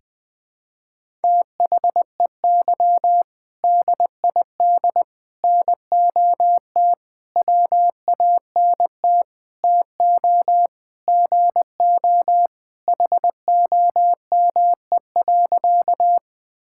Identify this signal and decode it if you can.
Morse code, heard as a keyed tone, about 20 wpm, 700 Hz. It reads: THEY DID NOT WANT TO GO HOME.
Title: Sentences with 5-Word Phrases - 20wpm